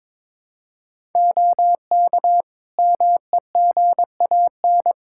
Morse code, heard as a keyed tone, about 21 wpm, 700 Hz. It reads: OK MEGAN
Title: Practice QSO - Exchange 4 - Ending - 22wpm